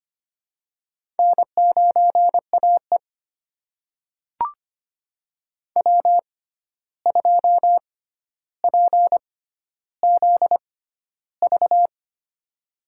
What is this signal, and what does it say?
Morse code, heard as a keyed tone, about 25 wpm, spaced as for 12 wpm, 700 Hz. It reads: N9AE W2PZV
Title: USA Call Signs - 6x Character Spacing - 25wpm